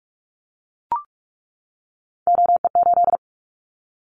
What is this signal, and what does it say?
Morse code, heard as a keyed tone, about 44 wpm, 700 Hz. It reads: OE8
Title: Callsign Prefixes for Top 100 DXCC - 45wpm